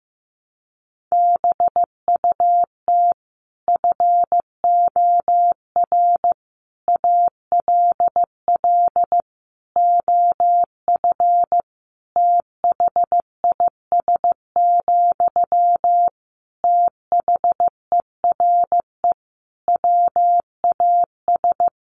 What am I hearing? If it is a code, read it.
Morse code, heard as a keyed tone, about 15 wpm, 700 Hz. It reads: BUT FOR ALL OF THIS, THERE WAS
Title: Sentences from Top 1300 Words 15wpm